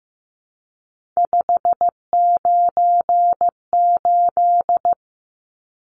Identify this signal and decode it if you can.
Morse code, heard as a keyed tone, about 15 wpm, 700 Hz. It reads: 598
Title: Numbers - 3-Digit 15wpm